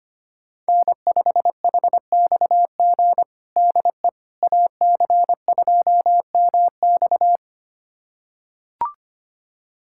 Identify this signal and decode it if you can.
Morse code, heard as a keyed tone, about 25 wpm, 700 Hz. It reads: N5HXG DE AC2MX
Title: USA Callsign DE Callsign - 5 Characters - 25wpm